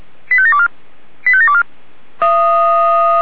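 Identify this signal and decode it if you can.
Five-tone selective call (ZVEI), heard as quick runs of five tones, tones of about 70 ms. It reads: 85724 85724
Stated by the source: ZVEI five-tone selective call received by SDR.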